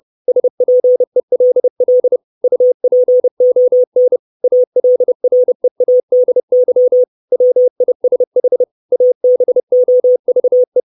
Morse code, heard as a keyed tone, about 30 wpm, 500 Hz. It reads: SPELL UPON ALREADY WISH ABOVE
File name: 632 Words 30WPM